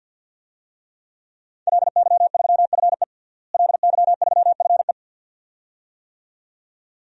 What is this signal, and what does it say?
Morse code, heard as a keyed tone, about 50 wpm, 700 Hz. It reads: LY3FE LY3FE
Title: DXCC-100 Call Signs - 5 Characters - Repeated 2x - 50wpm